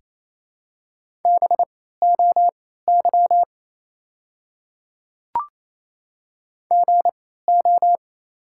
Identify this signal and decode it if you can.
Morse code, heard as a keyed tone, about 28 wpm, spaced as for 16 wpm, 700 Hz. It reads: BOY GO
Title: 3x Character Spacing - Top 200 Words - 28wpm